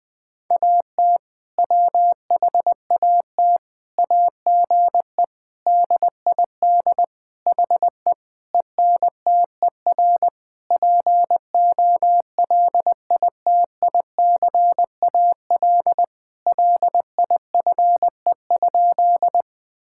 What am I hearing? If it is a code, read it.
Morse code, heard as a keyed tone, about 21 wpm, 700 Hz. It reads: AT WHAT AGE DID HE ENTER POLITICAL LIFE?
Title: Sentences from Top 1000 Words - Review Part 5 - 20wpm